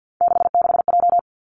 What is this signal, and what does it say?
Morse code, heard as a keyed tone, about 50 wpm, 700 Hz. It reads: 66J
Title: Three Challenging Letter-Number Combinations 50wpm